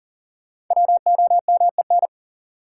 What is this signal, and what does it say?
Morse code, heard as a keyed tone, about 40 wpm, 700 Hz. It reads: WOMEN
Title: Sentences from Top 1300 Words - New Word Drill - 40wpm